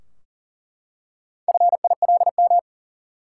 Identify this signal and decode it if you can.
Morse code, heard as a keyed tone, about 40 wpm, 700 Hz. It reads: FILM
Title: Four Letter Words - 40wpm